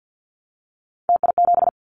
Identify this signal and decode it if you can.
Morse code, heard as a keyed tone, about 50 wpm, 690 Hz. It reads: TI7
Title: Callsign Prefixes for Top 100 DXCC - 50wpm